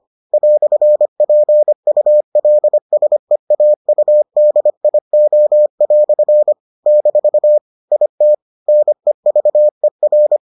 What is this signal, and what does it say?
Morse code, heard as a keyed tone, about 25 wpm, 600 Hz. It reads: "PULSEAUDIO" - IT NEVER